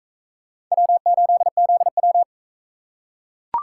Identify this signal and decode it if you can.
Morse code, heard as a keyed tone, about 41 wpm, 700 Hz. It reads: W8ZW